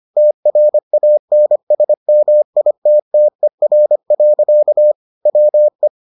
Morse code, heard as a keyed tone, about 25 wpm, 600 Hz. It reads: TRANSMITTER. WE